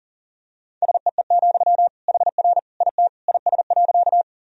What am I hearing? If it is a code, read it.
Morse code, heard as a keyed tone, about 40 wpm, 700 Hz. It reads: SEE, HR IT IS.
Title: Sentences from Top 200 Words and Abbreviations 40wpm